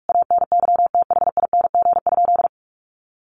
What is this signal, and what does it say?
Morse code, heard as a keyed tone, about 44 wpm, 700 Hz. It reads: ANYTHING?
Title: Sentences from Top 1100 Words 45wpm